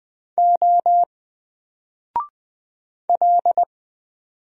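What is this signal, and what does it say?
Morse code, heard as a keyed tone, about 20 wpm, 700 Hz. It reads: O L
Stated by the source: Single letter and number Morse drill at 20 wpm, from N4LCD.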